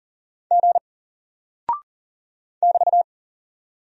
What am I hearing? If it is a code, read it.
Morse code, heard as a keyed tone, about 39 wpm, 700 Hz. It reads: G <BT>
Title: Single Character (Letter:Number:Punctuation:Prosign) Alt 40wpm